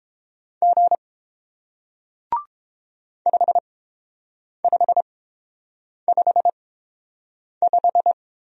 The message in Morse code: G 5555

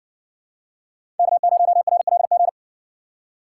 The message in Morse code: K0RLG

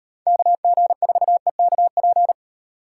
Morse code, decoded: KG4EKP

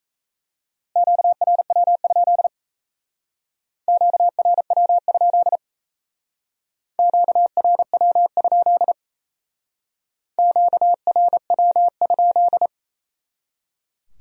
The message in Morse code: QRW? QRW? QRW? QRW?

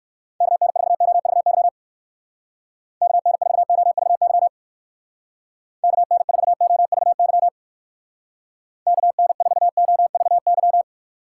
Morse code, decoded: KN4OVY KN4OVY KN4OVY KN4OVY

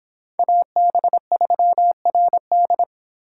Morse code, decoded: AB3RD

